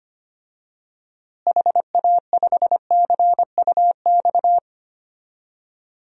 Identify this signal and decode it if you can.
Morse code, heard as a keyed tone, about 25 wpm, 700 Hz. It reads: HA5CUX
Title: DXCC-100 Call Signs - 6 Characters - Repeated 2x - 25wpm